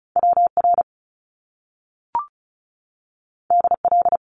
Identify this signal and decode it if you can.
Morse code, heard as a keyed tone, about 36 wpm, 700 Hz. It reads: WR DL
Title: Top Two Letter Combinations in Words 35wpm